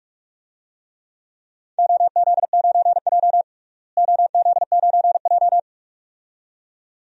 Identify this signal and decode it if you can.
Morse code, heard as a keyed tone, about 46 wpm, 700 Hz. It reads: OZ9J OZ9J
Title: DXCC-100 Call Signs - 4 Characters - Repeated 2x - 45wpm